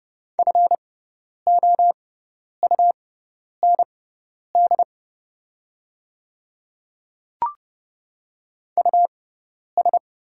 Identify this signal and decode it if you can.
Morse code, heard as a keyed tone, about 30 wpm, spaced as for 10 wpm, 700 Hz. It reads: FOUND US
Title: 6x Character Spacing - Top 300 Words - 30wpm